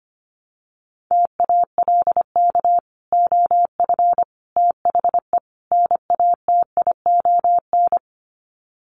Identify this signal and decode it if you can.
Morse code, heard as a keyed tone, about 25 wpm, 700 Hz. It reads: TALK OF THE NATION
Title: Sets of 4 Words 25wpm